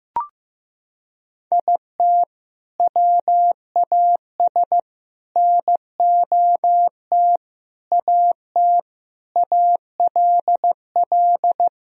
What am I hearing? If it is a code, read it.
Morse code, heard as a keyed tone, about 15 wpm, 700 Hz. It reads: IT WAS NOT AT ALL
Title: Sentences from Top 1000 Words - Review Part 4 - 15wpm